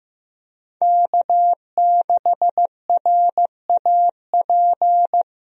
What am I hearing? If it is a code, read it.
Morse code, heard as a keyed tone, about 15 wpm, 700 Hz. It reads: K6RAP
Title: USA Call Signs - 5 Characters - 15wpm